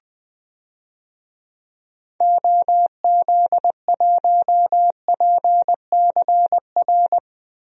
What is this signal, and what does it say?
Morse code, heard as a keyed tone, about 20 wpm, 700 Hz. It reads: OZ1PCR
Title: DXCC-100 Call Signs - 6 Characters - Repeated 2x - 20wpm